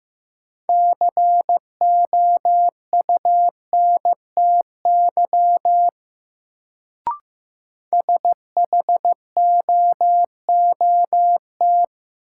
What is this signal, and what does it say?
Morse code, heard as a keyed tone, about 15 wpm, 700 Hz. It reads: COUNTY SHOOT
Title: Sentences from Top 1200 Words - New Word Drill - 15wpm